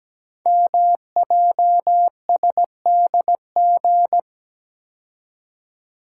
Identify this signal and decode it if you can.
Morse code, heard as a keyed tone, about 17 wpm, 700 Hz. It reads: MJSDG
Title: Code Groups - 5 Letters - 17wpm